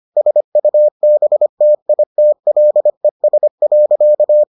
Morse code, heard as a keyed tone, about 25 wpm, 600 Hz. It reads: SUBTITLES.